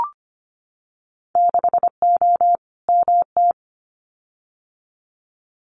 Morse code, heard as a keyed tone, about 25 wpm, 700 Hz. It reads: 6O MT
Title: Winter Field Day Exchanges Encore 25wpm